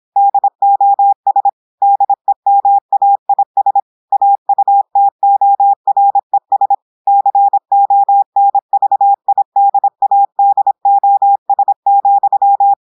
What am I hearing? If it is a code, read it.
Morse code, heard as a keyed tone, about 26 wpm, 800 Hz. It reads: DOS DEMAIS AUTORES CONVIDADOS,
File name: audios / portugais / portugais texte en clair 1 vitesse 26 mots minutes